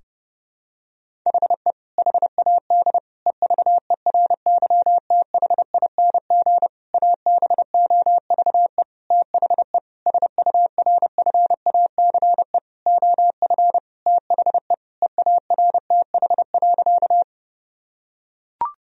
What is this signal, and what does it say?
Morse code, heard as a keyed tone, about 30 wpm, 700 Hz. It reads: HE HAD EVERYTHING ABOVE THE SURFACE OF THE EARTH.
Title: Sentences from Top 1000 Words - Review Part 3 - 30wpm